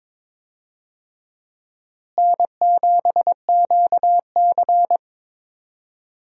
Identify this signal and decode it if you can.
Morse code, heard as a keyed tone, about 22 wpm, 700 Hz. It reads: N7QC
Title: USA Call Signs - 4 Characters - Repeated 2x - 22wpm